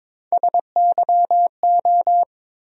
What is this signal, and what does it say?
Morse code, heard as a keyed tone, about 21 wpm, 700 Hz. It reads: SYO
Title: USA Call Sign Postfixes - 3 Letters 22wpm